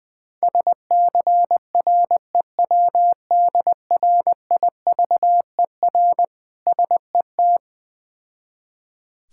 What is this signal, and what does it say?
Morse code, heard as a keyed tone, about 21 wpm, 700 Hz. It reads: SCREWDRIVER SET
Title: Difficult Words with Hints - 20wpm